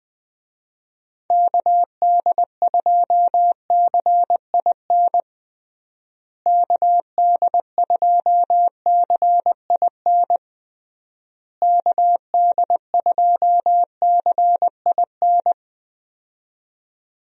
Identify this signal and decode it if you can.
Morse code, heard as a keyed tone, about 20 wpm, 700 Hz. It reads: KD2CIN KD2CIN KD2CIN
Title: USA Call Signs - 6 Characters - Repeated 3x - 20wpm